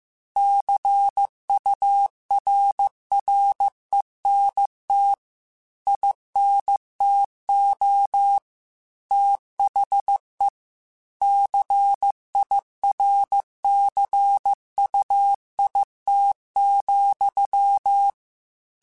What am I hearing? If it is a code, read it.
Morse code, heard as a keyed tone, about 15 wpm, 800 Hz. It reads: CURRENT INTO THE CIRCUIT,